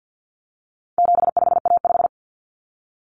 Morse code, heard as a keyed tone, about 50 wpm, 700 Hz. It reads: 75A5
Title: Common Cut Numbers - 4-Digit 50wpm